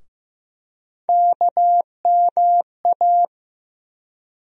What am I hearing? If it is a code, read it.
Morse code, heard as a keyed tone, about 15 wpm, 700 Hz. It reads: KMA
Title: USA Call Sign Postfixes - 3 Letters 15wpm